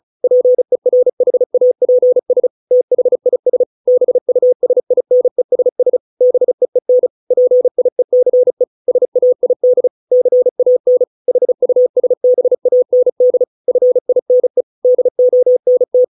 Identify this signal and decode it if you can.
Morse code, heard as a keyed tone, about 34 wpm, 500 Hz. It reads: PERHAPS THIS BUSINESS BEEN PIECE SAID CAN HUSBAND FINE DONT